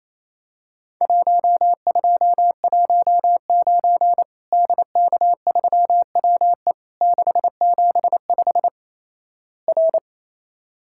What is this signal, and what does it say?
Morse code, heard as a keyed tone, about 28 wpm, 700 Hz. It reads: 1219 DK3WE 675 R
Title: Worked All Europe DX Contest - QTC Exchange - 28wpm